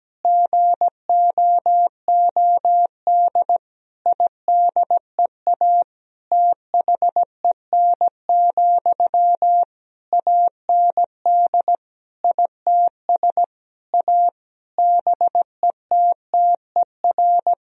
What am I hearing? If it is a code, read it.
Morse code, heard as a keyed tone, about 17 wpm, 700 Hz. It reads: GOOD IDEA THEN, AND ITS A BETTER
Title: Sentences from Top 1000 Words - Review Part 4 - 17wpm